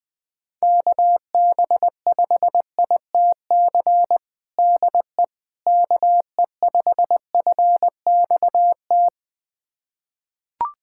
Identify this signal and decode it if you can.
Morse code, heard as a keyed tone, about 20 wpm, 700 Hz. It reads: KB5ITC DE KE5FXT
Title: USA Callsign DE Callsign - 6 Characters - 20wpm